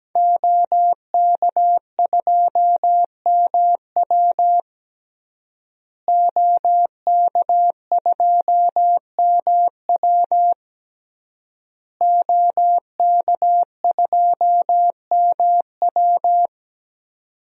Morse code, heard as a keyed tone, about 17 wpm, 700 Hz. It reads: OK2MW OK2MW OK2MW
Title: DXCC-100 Call Signs - 5 Characters - Repeated 3x - 17wpm